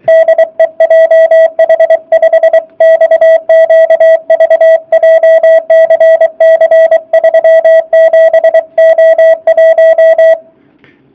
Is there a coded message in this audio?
Yes: Morse code from 0.1 to 10.3 s, heard as a keyed tone, about 24 wpm, 660 Hz. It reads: DE JH5XQV JCC 37O1